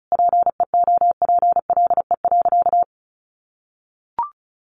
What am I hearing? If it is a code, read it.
Morse code, heard as a keyed tone, about 34 wpm, 700 Hz. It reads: PEOPLE.
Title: Sentences from Top 1500 Words 35wpm